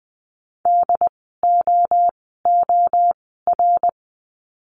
Morse code, heard as a keyed tone, about 20 wpm, spaced as for 14 wpm, 700 Hz. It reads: DOOR
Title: 075 Lesson 13f - Letter L - Words 20wpm